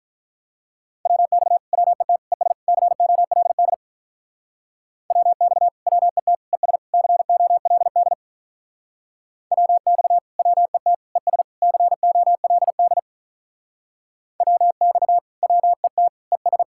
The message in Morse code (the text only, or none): WX WET ES COLD WX WET ES COLD WX WET ES COLD WX WET ES